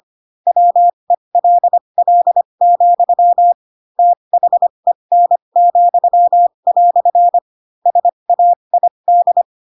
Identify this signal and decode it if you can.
Morse code, heard as a keyed tone, about 25 wpm, spaced as for 22 wpm, 700 Hz. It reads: WELL, THEN," SAID
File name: Pinocchio / Pinocchio ch7 18 25 ik1hge